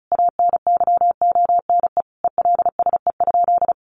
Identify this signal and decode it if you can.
Morse code, heard as a keyed tone, about 34 wpm, 700 Hz. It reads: ANYONE ELSE?
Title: Sentences from Top 900 Words 35wpm